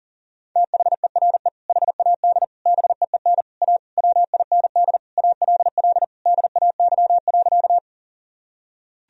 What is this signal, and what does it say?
Morse code, heard as a keyed tone, about 40 wpm, 700 Hz. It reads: THERE HAD BEEN A WIND ALL DAY.
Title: Sentences from Top 1000 Words - Review Part 2 - 40wpm